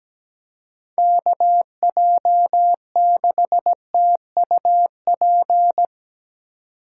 Morse code, heard as a keyed tone, about 19 wpm, 700 Hz. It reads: KJ6TUP
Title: Speed Racing - USA 2x3 Call Signs 17wpm